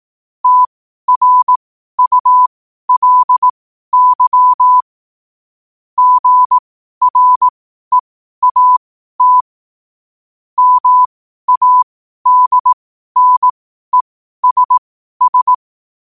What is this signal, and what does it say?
Morse code, heard as a keyed tone, about 18 wpm, spaced as for 12 wpm, 1000 Hz. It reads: TRULY GREAT MADNESS
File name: qotdfiles / 12wpm